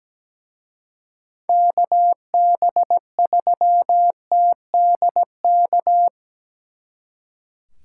Morse code, heard as a keyed tone, about 20 wpm, 700 Hz. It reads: KB3TDK
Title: Speed Racing - USA 2x3 Call Signs 17wpm